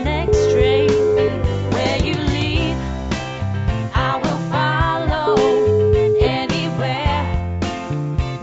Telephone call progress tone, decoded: ringback tone

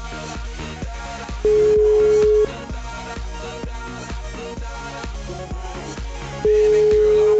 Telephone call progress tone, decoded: ringback tone

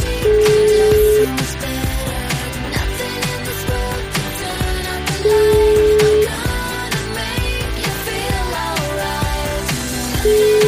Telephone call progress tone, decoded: ringback tone

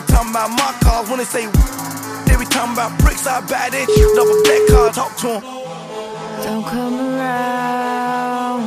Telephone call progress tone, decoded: ringback tone

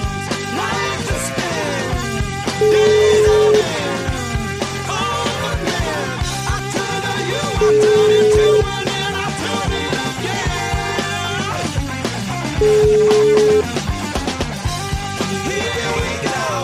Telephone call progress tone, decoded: ringback tone

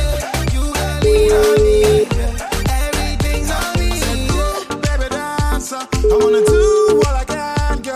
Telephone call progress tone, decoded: ringback tone

